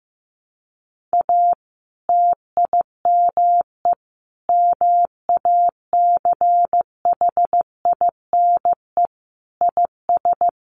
Morse code, heard as a keyed tone, about 15 wpm, 700 Hz. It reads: A TIME MACHINE IS
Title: A Sound Of Thunder 15wpm